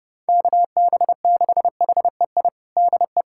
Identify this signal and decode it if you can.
Morse code, heard as a keyed tone, about 29 wpm, 700 Hz. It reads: KB6HEI DE
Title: USA Callsign DE Callsign - 6 Characters - 30wpm